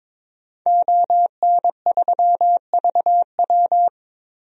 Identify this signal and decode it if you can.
Morse code, heard as a keyed tone, about 22 wpm, 700 Hz. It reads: ON3VW
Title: DXCC-100 Call Signs - 5 Characters - 22wpm